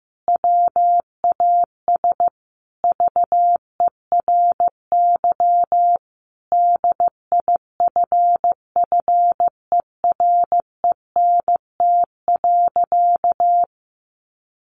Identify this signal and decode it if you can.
Morse code, heard as a keyed tone, about 15 wpm, 700 Hz. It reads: WAS VERY DIFFERENT.
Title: Sentences from Top 1300 Words 15wpm